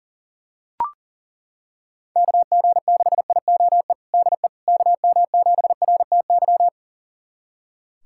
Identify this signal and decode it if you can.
Morse code, heard as a keyed tone, about 40 wpm, 700 Hz. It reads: KG6IOE DE KM7RTY
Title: USA Callsign DE Callsign 40wpm